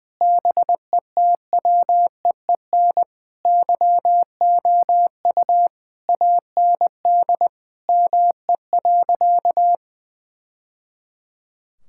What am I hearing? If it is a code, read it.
Morse code, heard as a keyed tone, about 20 wpm, 700 Hz. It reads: BETWEEN YOU AND ME.
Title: Sentences from Top 1000 Words - Review Part 6 - 20wpm